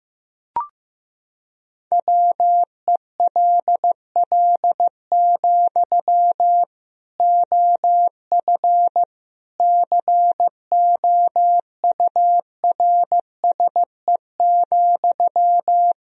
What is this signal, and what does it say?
Morse code, heard as a keyed tone, about 15 wpm, 700 Hz. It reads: WELL, OF COURSE,
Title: Sentences from Top 1000 Words - Review Part 1 - 15wpm